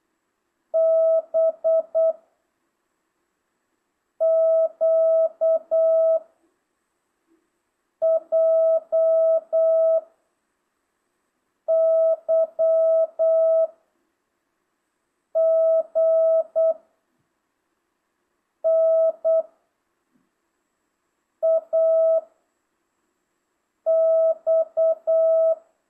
Morse code, heard as a keyed tone, about 8 wpm, 650 Hz. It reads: B Q J Y G N A X
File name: jh8cbh / 2025cw